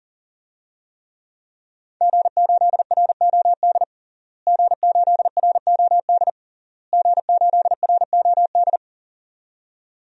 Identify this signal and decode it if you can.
Morse code, heard as a keyed tone, about 37 wpm, 700 Hz. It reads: G8ROD G8ROD G8ROD
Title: DXCC-100 Call Signs - 5 Characters - Repeated 3x - 40wpm